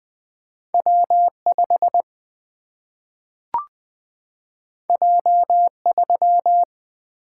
Morse code, heard as a keyed tone, about 20 wpm, 700 Hz. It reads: W5 J3
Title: Single-Letter Single-Number 20wpm